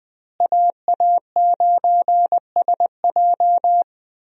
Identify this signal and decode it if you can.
Morse code, heard as a keyed tone, about 21 wpm, 700 Hz. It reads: AA9SJ
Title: USA Call Signs - 5 Characters - Repeated 3x - 20wpm